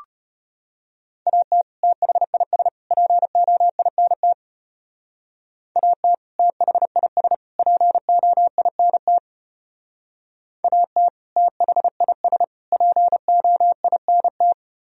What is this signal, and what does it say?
Morse code, heard as a keyed tone, about 34 wpm, 700 Hz. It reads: AT THIS POINT AT THIS POINT AT THIS POINT